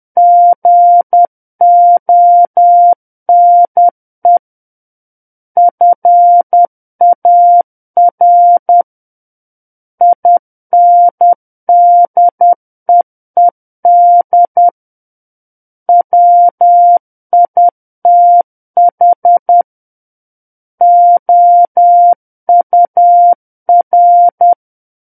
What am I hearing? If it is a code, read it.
Morse code, heard as a keyed tone, about 10 wpm, 700 Hz. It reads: GONE FAR INDEED WITH OUR
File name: Book One / 01-Chapter 1 - 10 WPM